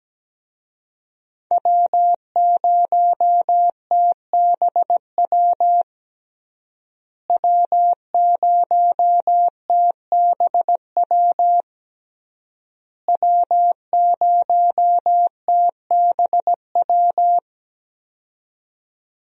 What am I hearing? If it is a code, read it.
Morse code, heard as a keyed tone, about 17 wpm, 700 Hz. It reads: W0TBW W0TBW W0TBW
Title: USA Call Signs - 5 Characters - Repeated 3x - 17wpm